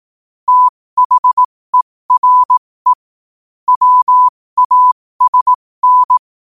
Morse code, heard as a keyed tone, about 18 wpm, spaced as for 15 wpm, 1000 Hz. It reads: THERE WASN
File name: qotdfiles / 15wpm